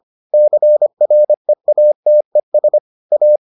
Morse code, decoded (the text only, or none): CREATES A